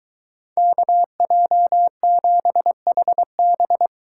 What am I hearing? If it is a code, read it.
Morse code, heard as a keyed tone, about 23 wpm, 700 Hz. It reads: KJ7HB